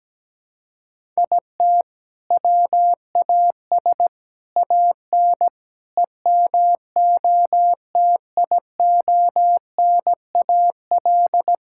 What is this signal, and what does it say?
Morse code, heard as a keyed tone, about 17 wpm, 700 Hz. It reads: IT WAS AN EMOTIONAL
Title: Sentences from Top 1000 Words - Review Part 5 - 17wpm